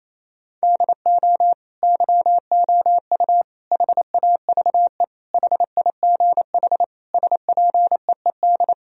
Morse code, heard as a keyed tone, about 28 wpm, 700 Hz. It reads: DO YOU HAVE HIGH SPEED
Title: Sentences from Top 1000 Words - Review Part 4 - 28wpm